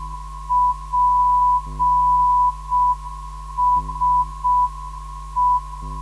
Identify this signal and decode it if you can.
Morse code, heard as a keyed tone, about 6 wpm, 990 Hz. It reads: PSE
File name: ndb zvok / DI2BO 440 - 2